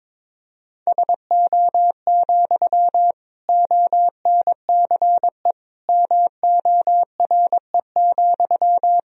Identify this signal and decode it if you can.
Morse code, heard as a keyed tone, about 21 wpm, 700 Hz. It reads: SO, ONCE MORE,
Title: Sentences from Top 300 Words and Abbreviations 22wpm